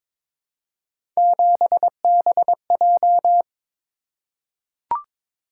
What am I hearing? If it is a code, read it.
Morse code, heard as a keyed tone, about 22 wpm, 700 Hz. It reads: 7BJ
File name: practice / 22wpm / Three Letter-Number Combinations - 22wpm